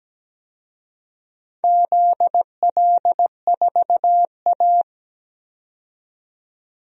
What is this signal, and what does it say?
Morse code, heard as a keyed tone, about 17 wpm, 700 Hz. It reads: ZL4A
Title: DXCC-100 Call Signs - 4 Characters - 17wpm